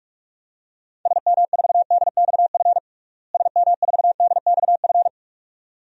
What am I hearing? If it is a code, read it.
Morse code, heard as a keyed tone, about 46 wpm, 700 Hz. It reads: SM4DXF SM4DXF
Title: DXCC-100 Call Signs - 6 Characters - Repeated 2x - 45wpm